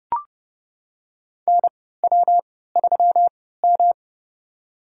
Morse code, heard as a keyed tone, about 30 wpm, spaced as for 17 wpm, 700 Hz. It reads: NW3M
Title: USA Call Signs - 3x Character Spacing - 30wpm